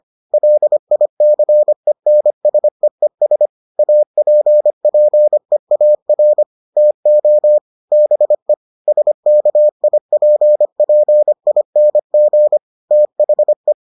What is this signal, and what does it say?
Morse code, heard as a keyed tone, about 25 wpm, 600 Hz. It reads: LICENSEES APPEAR TO BE SKIPPING THE